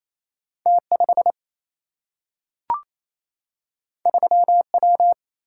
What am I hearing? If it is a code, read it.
Morse code, heard as a keyed tone, about 28 wpm, 700 Hz. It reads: T 5 3 W
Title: Single-Letter Single-Number 28wpm